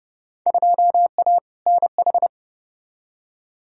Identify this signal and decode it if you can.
Morse code, heard as a keyed tone, about 30 wpm, 700 Hz. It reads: 2A NH
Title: Field Day Exchanges Encore - 30wpm